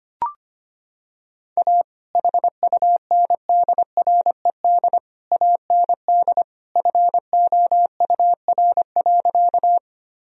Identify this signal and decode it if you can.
Morse code, heard as a keyed tone, about 25 wpm, 700 Hz. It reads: A HUNDRED AND FOUR.
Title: Sentences from Top 1000 Words - Review Part 2 - 25wpm